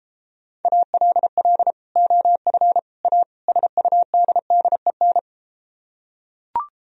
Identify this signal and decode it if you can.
Morse code, heard as a keyed tone, about 33 wpm, 700 Hz. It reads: ALL OF A SUDDEN